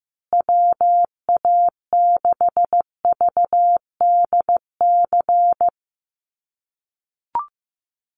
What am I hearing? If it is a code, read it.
Morse code, heard as a keyed tone, about 15 wpm, 700 Hz. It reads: WA6VDC
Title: USA Call Signs Repeated 2x 15wpm